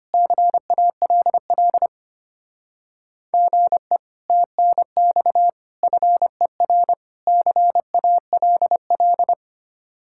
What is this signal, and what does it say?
Morse code, heard as a keyed tone, about 30 wpm, 700 Hz. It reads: CALL GE TNX FER CALL
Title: Speed-Racing Common QSO Phrases Encore - 25wpm